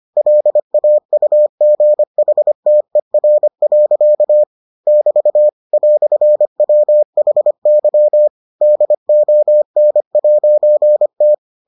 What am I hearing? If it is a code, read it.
Morse code, heard as a keyed tone, about 25 wpm, 600 Hz. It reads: LAUGHTER. <BT> "WHY DON'T